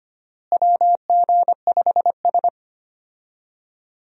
Morse code, heard as a keyed tone, about 25 wpm, 700 Hz. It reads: WG5S
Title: USA Call Signs - 4 Characters - Repeated 2x - 25wpm